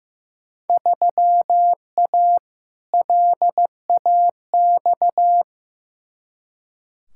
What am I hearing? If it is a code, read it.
Morse code, heard as a keyed tone, about 16 wpm, 700 Hz. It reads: 3A LAX
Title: Field Day Exchanges - 15wpm